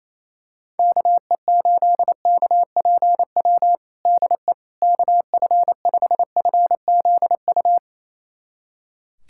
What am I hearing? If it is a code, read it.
Morse code, heard as a keyed tone, about 28 wpm, 700 Hz. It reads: KE8KPW DE KF5FZU
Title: USA Callsign DE Callsign 28wpm